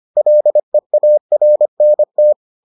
Morse code, heard as a keyed tone, about 25 wpm, 600 Hz. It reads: LEARNT